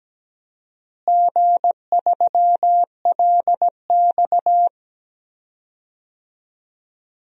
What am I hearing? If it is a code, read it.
Morse code, heard as a keyed tone, about 17 wpm, 700 Hz. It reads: G3LX
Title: DXCC-100 Call Signs - 4 Characters - Repeated 2x - 17wpm